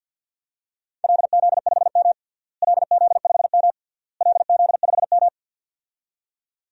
Morse code, heard as a keyed tone, about 48 wpm, 700 Hz. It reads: LZ5M LZ5M LZ5M
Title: DXCC-100 Call Signs - 4 Characters - Repeated 3x - 50wpm